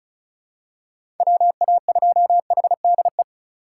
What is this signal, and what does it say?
Morse code, heard as a keyed tone, about 34 wpm, 700 Hz. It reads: WA2HDE
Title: USA Call Signs - 6 Characters - 3x Character Spacing - 35wpm